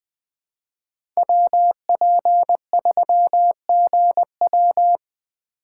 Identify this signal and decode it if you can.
Morse code, heard as a keyed tone, about 21 wpm, 700 Hz. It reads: WP3GW
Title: USA Call Signs - 5 Characters - 2x Character Spacing - 20wpm